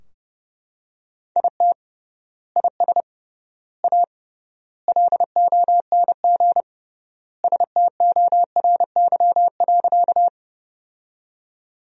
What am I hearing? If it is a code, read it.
Morse code, heard as a keyed tone, about 30 wpm, 700 Hz. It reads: IT IS A LONG STORY.
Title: 3x Word-Spacing - Sentences from Top 300 Words - 30wpm